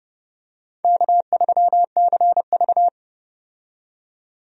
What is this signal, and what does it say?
Morse code, heard as a keyed tone, about 30 wpm, 700 Hz. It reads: K3CV
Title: USA Call Signs - 4 Characters - 30wpm